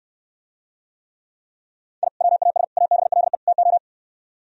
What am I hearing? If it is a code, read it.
Morse code, heard as a keyed tone, about 68 wpm, 700 Hz. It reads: I WAS ABLE TO